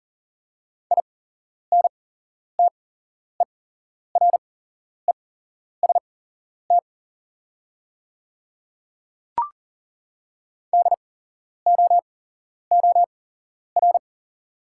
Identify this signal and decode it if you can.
Morse code, heard as a keyed tone, about 39 wpm, spaced as for 11 wpm, 690 Hz. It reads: INTEREST DOOR
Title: 8x Character Spacing - Top 500 Words - 40wpm